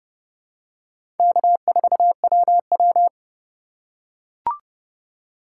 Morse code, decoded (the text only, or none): K4WW